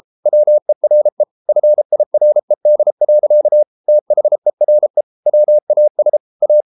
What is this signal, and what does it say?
Morse code, heard as a keyed tone, about 33 wpm, 600 Hz. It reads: WERE FIRED. THERE WAS A